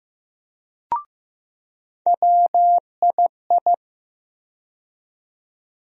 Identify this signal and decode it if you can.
Morse code, heard as a keyed tone, about 15 wpm, 700 Hz. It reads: WII
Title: USA Call Sign Postfixes - 3 Letters 15wpm